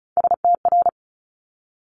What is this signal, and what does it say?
Morse code, heard as a keyed tone, about 36 wpm, 700 Hz. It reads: STR
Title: Multi-letter Phonemes Quick Fire 35wpm